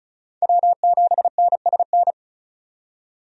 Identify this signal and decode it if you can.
Morse code, heard as a keyed tone, about 36 wpm, 700 Hz. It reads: W7NSN
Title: USA Call Signs - 5 Characters - 35wpm